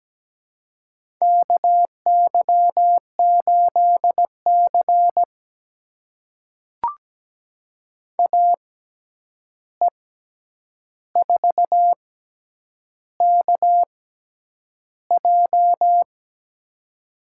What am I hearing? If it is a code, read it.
Morse code, heard as a keyed tone, about 17 wpm, spaced as for 8 wpm, 700 Hz. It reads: KY8C AE4KJ
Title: USA Call Signs - 6x Character Spacing - 17wpm